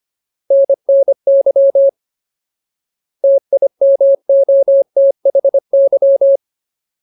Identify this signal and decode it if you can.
Morse code, heard as a keyed tone, about 25 wpm, 550 Hz. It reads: NNY TIMOTHY
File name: Ultimate Word List / 25WPM Ultimate